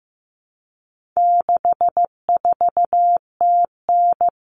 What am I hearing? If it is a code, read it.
Morse code, heard as a keyed tone, about 15 wpm, 700 Hz. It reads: 64TN
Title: Common Cut Numbers - 4-Digit 15wpm